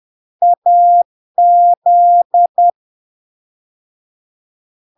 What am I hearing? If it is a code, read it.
Morse code, heard as a keyed tone, about 10 wpm, 700 Hz. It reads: AZ
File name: LARK-FD-2022-Log-in-CW-10wpm-700-Hz